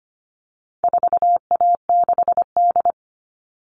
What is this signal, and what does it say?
Morse code, heard as a keyed tone, about 25 wpm, 700 Hz. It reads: 4A6D
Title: All Cut Numbers - 4-Digit 25wpm